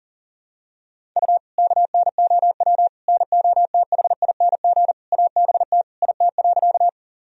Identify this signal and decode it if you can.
Morse code, heard as a keyed tone, about 40 wpm, 700 Hz. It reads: U KNOW NOTHING ABT IT.
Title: Sentences from Top 500 Words and Abbreviations 40wpm